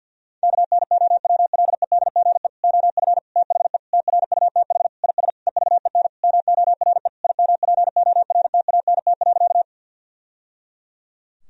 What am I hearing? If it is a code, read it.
Morse code, heard as a keyed tone, about 47 wpm, 700 Hz. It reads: KNOWLEDGE OF THE TRUTH IS EVEN MORE IMPORTANT.
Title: Sentences from Top 1000 Words - Review Part 3 - 50wpm